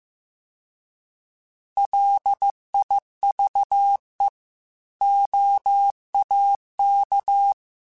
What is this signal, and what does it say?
Morse code, heard as a keyed tone, about 15 wpm, 800 Hz. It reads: LIVE OAK
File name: fromSMs / FromSMs mp3s / Cities@13